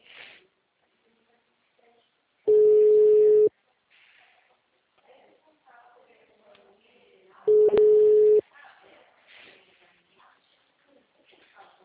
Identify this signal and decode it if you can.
Ringback tone (Europe, 425 Hz), with speech.